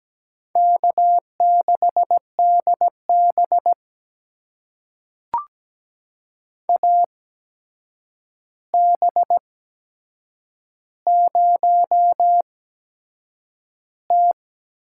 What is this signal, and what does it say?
Morse code, heard as a keyed tone, about 17 wpm, spaced as for 7 wpm, 700 Hz. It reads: K6DB AB0T
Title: USA Call Signs - 4 Characters - 8x Character Spacing - 17wpm